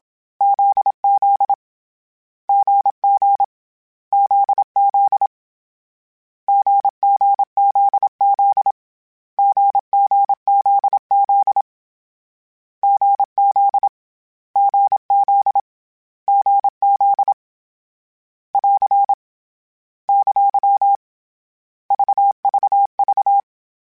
Morse code, heard as a keyed tone, about 27 wpm, 800 Hz. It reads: ZZ GG ZZ GGZZ GGZZ GZ GZ GZ <AR> ! VVV
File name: Short Runs / Short Runs-mp3s / GZ GZ7 @25